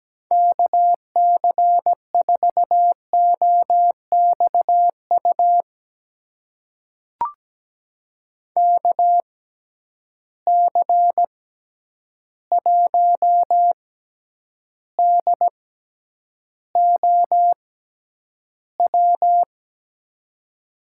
Morse code, heard as a keyed tone, about 17 wpm, spaced as for 9 wpm, 700 Hz. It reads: KC4OXU KC1DOW
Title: USA Call Signs - 6 Characters - 6x Character Spacing - 17wpm